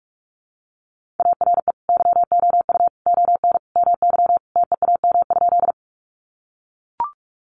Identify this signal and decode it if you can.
Morse code, heard as a keyed tone, about 44 wpm, 700 Hz. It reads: ARE YOU ON MY TEAM?
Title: Sentences from Top 1500 Words 45wpm